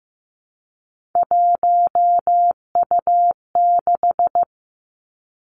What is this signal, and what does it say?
Morse code, heard as a keyed tone, about 15 wpm, 700 Hz. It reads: 1U6
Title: Three Challenging Letter-Number Combinations 15wpm